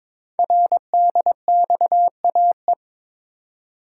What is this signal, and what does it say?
Morse code, heard as a keyed tone, about 21 wpm, 700 Hz. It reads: RDXAE
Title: Code Groups - 5 Letters - 22wpm